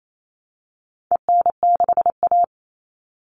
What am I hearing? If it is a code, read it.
Morse code, heard as a keyed tone, about 28 wpm, 700 Hz. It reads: EN6A
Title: All Cut Numbers - 4-Digit 28wpm